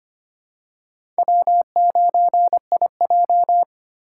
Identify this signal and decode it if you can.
Morse code, heard as a keyed tone, about 25 wpm, 700 Hz. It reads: W9IJ
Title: USA Call Signs - 2x Character Spacing - 25wpm